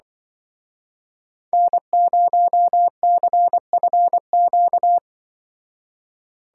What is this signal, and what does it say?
Morse code, heard as a keyed tone, about 24 wpm, 700 Hz. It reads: N0CFQ
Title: Speed Racing - USA 1x3 Call Signs 20wpm